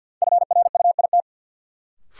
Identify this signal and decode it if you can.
Morse code, heard as a keyed tone, about 52 wpm, 700 Hz. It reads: FRUIT
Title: Sentences from Top 1300 Words - New Word Drill - 50wpm